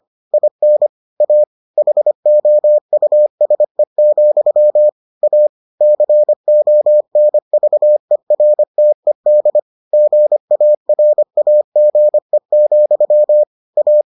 Morse code, heard as a keyed tone, about 25 wpm, 600 Hz. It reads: IN A HOUSE, A CONVERTED GARAGE, A